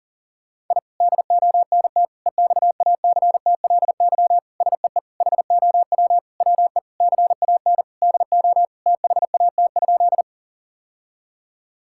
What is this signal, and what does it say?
Morse code, heard as a keyed tone, about 40 wpm, 700 Hz. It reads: I DONT EXACTLY SEE HOW WE CAN DO THAT?
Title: Sentences from Top 1000 Words - Review Part 6 - 40wpm